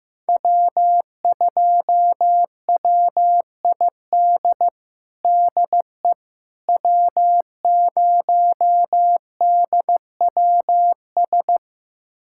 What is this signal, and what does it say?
Morse code, heard as a keyed tone, about 15 wpm, 700 Hz. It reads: W2WID DE W0DWS
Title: USA Callsign DE Callsign - 5 Characters - 15wpm